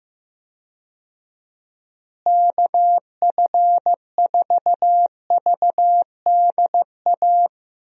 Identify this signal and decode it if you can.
Morse code, heard as a keyed tone, about 15 wpm, 700 Hz. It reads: KF4VDA
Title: USA Call Signs - 6 Characters - Repeated 2x - 15wpm